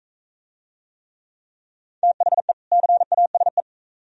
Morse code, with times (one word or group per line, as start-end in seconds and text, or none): 2.0-2.5 THE
2.7-3.6 CASE